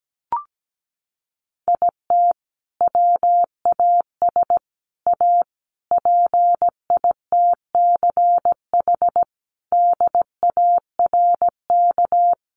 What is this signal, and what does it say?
Morse code, heard as a keyed tone, about 17 wpm, 700 Hz. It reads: IT WAS A PITCH DARK
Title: Sentences from Top 1300 Words 17wpm